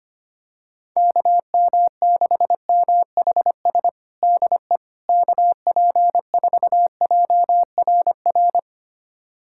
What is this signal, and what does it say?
Morse code, heard as a keyed tone, about 25 wpm, 700 Hz. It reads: KM6MHS DE KP4JRR
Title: USA Callsign DE Callsign - 6 Characters - 25wpm